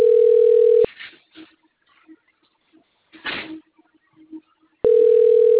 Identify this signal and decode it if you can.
Ringback tone (North America), 440 + 480 Hz.